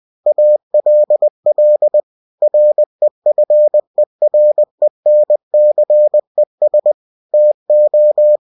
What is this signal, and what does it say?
Morse code, heard as a keyed tone, about 20 wpm, 600 Hz. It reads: ALL REFERENCES TO
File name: cw-books / winnie-the-pooh / winnie-20wpm-page-0114